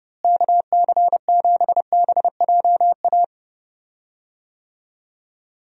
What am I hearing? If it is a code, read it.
Morse code, heard as a keyed tone, about 30 wpm, 700 Hz. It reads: KC7BJA
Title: USA Call Signs - 6 Characters - Repeated 3x - 30wpm